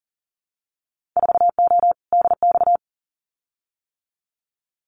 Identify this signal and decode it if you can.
Morse code, heard as a keyed tone, about 40 wpm, 700 Hz. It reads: 4O DX
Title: Winter Field Day Exchanges Encore 40wpm